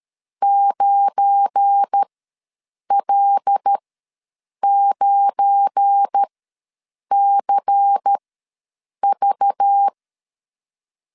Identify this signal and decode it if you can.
Morse code, heard as a keyed tone, about 13 wpm, spaced as for 7 wpm, 800 Hz. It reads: 9L9CV
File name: LEZIONI / 7 GF9Z / 7 GF9 ws 12wpm 3344